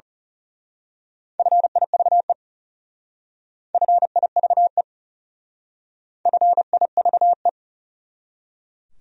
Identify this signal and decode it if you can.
Morse code, heard as a keyed tone, about 34 wpm, 700 Hz. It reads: FIVE FIVE FIVE
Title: Speed-Racing Top 500 Words 30wpm